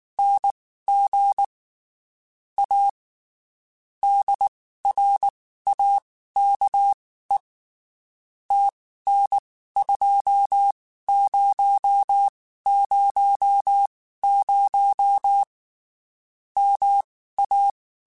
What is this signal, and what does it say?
Morse code, heard as a keyed tone, about 19 wpm, spaced as for 14 wpm, 800 Hz. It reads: NG A DRAKE TN2000 MA
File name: QSOs / QSOs-mp3s / WD0FBT 1@18